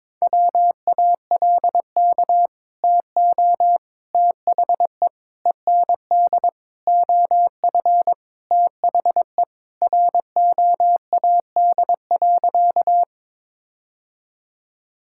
Morse code, heard as a keyed tone, about 22 wpm, 700 Hz. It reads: WALK TO THE END OF THE ROAD.
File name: practice / 22wpm / Sentences with 4-Word Phrases - 22wpm